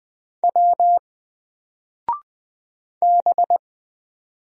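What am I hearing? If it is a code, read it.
Morse code, heard as a keyed tone, about 20 wpm, 700 Hz. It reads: W B
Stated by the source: Single letter and number Morse drill at 20 wpm, from N4LCD.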